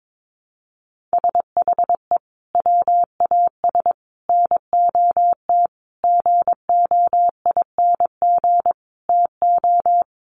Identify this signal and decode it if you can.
Morse code, heard as a keyed tone, about 22 wpm, 700 Hz. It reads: SHE WAS NOT GOING TO
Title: Sets of 5 Words Encore2 22wpm